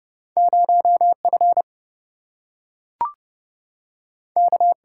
Morse code, heard as a keyed tone, about 30 wpm, 700 Hz. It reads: 0 F K
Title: Single-Letter Single-Number 30wpm